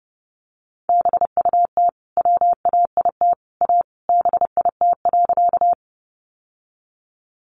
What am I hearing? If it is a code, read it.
Morse code, heard as a keyed tone, about 30 wpm, 700 Hz. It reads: BUT WAIT A BIT.
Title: Sentences from Top 900 Words 30wpm